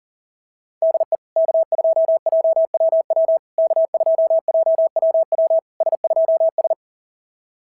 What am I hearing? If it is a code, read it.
Morse code, heard as a keyed tone, about 40 wpm, 690 Hz. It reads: DE K2JWW K2JWW S2S
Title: SOTA QSO Practice 40wpm